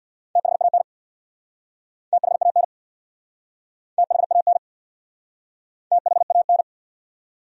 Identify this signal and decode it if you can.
Morse code, heard as a keyed tone, about 69 wpm, 700 Hz. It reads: THAN THAN THAN THAN